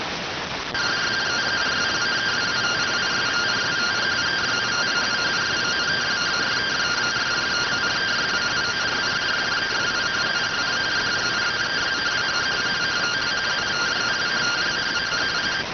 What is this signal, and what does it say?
Two-tone teleprinter signal, about 170 Hz shift.